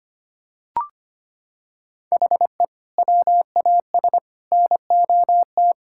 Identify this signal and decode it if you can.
Morse code, heard as a keyed tone, about 25 wpm, 700 Hz. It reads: HE WAS NOT
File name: practice / 25wpm / Sentences with 3-Word Phrases - 25wpm